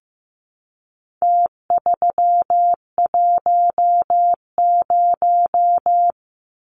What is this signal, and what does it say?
Morse code, heard as a keyed tone, about 15 wpm, 700 Hz. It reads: T310
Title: Numbers - Contest Style - 4-Digit 15wpm